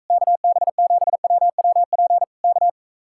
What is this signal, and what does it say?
Morse code, heard as a keyed tone, about 44 wpm, 700 Hz. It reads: KB7WWP K